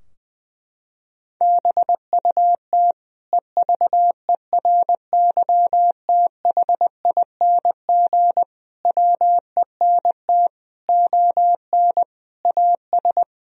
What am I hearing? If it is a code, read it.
Morse code, heard as a keyed tone, about 20 wpm, 700 Hz. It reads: BUT EVERYTHING WENT ON AS
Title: Sentences from Top 1000 Words - Review Part 2 - 20wpm